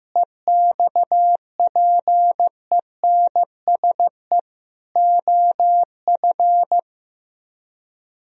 Morse code, heard as a keyed tone, about 16 wpm, 700 Hz. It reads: EXPENSE OF
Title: Speed-Racing - Sets of 4 Words 15wpm